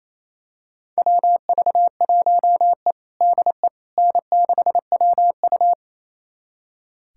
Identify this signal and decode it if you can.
Morse code, heard as a keyed tone, about 27 wpm, 700 Hz. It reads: WV1E DE N6WU
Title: USA Callsign DE Callsign - 4 Characters - 28wpm